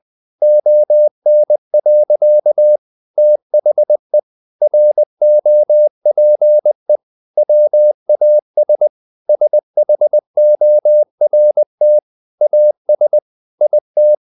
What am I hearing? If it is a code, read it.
Morse code, heard as a keyed tone, about 20 wpm, 600 Hz. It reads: ON. THE ROPE WAS SHORT AS IT
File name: the old man and the sea part 2 20wpm